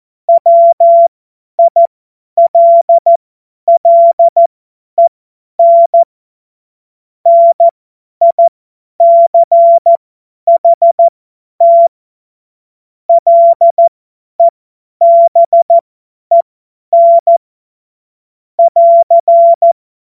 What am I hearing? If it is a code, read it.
Morse code, heard as a keyed tone, about 14 wpm, spaced as for 10 wpm, 680 Hz. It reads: WILLEN NICHT LEBEN <AR>
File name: Morsekurs / Witze / Witz 17. Januar 2023 50 F70